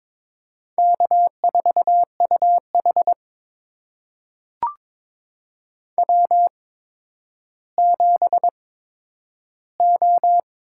Morse code, heard as a keyed tone, about 22 wpm, spaced as for 11 wpm, 700 Hz. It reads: K4UH W7O
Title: USA Call Signs - 4 Characters - 8x Character Spacing - 22wpm